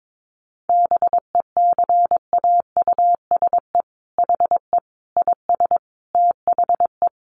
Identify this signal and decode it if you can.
Morse code, heard as a keyed tone, about 22 wpm, 700 Hz. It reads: BECAUSE HE IS THE
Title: Sentences from Top 1500 Words 22wpm